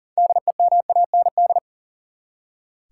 Morse code, heard as a keyed tone, about 40 wpm, 700 Hz. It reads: DEMAND
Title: Six Letter Words - 40wpm